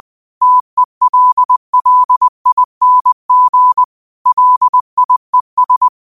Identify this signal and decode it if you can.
Morse code, heard as a keyed tone, about 20 wpm, 1000 Hz. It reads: TELLING LIES